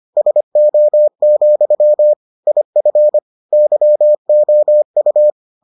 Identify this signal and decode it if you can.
Morse code, heard as a keyed tone, about 25 wpm, 600 Hz. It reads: SO, IF YOU